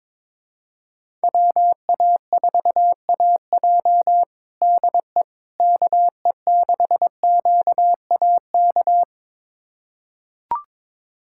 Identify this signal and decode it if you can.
Morse code, heard as a keyed tone, about 22 wpm, 700 Hz. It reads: WA4AJ DE KE6QAK
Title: USA Callsign DE Callsign 22wpm